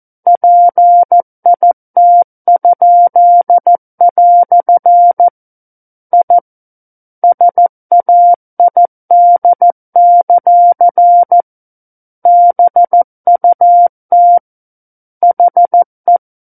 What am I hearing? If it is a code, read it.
Morse code, heard as a keyed tone, about 14 wpm, 700 Hz. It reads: PIT?" I SAID; BUT HE
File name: Book One / 05-Chapter 5 - 14 WPM